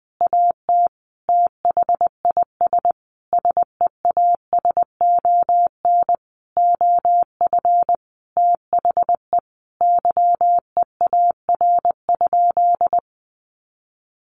Morse code, heard as a keyed tone, about 19 wpm, 700 Hz. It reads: AT THIS SEASON OF THE YEAR?
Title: Sentences from Top 1400 Words 20wpm